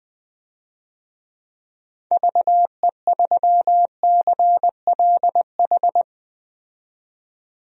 Morse code, heard as a keyed tone, about 20 wpm, 700 Hz. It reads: VE3CLH
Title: DXCC-100 Call Signs - 6 Characters - 20wpm